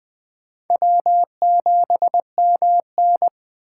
Morse code, heard as a keyed tone, about 21 wpm, 700 Hz. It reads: W7MN
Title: USA Call Signs - 4 Characters - 6x Character Spacing - 20wpm